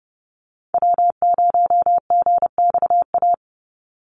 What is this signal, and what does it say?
Morse code, heard as a keyed tone, about 30 wpm, 700 Hz. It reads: W0GXA
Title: Call Signs Repeated 2x - 30wpm